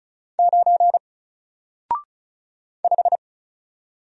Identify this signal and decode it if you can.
Morse code, heard as a keyed tone, about 35 wpm, 700 Hz. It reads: 9 5
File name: Single-Letter-Number-35wpm